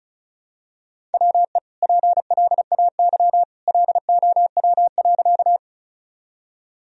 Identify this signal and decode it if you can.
Morse code, heard as a keyed tone, about 34 wpm, 700 Hz. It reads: WE PLAY LOW.
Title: Sentences from Top 1000 Words - Review Part 1 - 35wpm